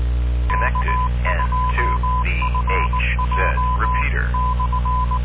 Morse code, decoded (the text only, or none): K2FJ /